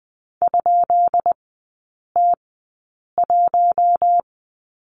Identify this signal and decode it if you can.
Morse code, heard as a keyed tone, about 20 wpm, spaced as for 8 wpm, 690 Hz. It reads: ?T1
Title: 133 Lesson 22e - Question Mark - Three Characters 20wpm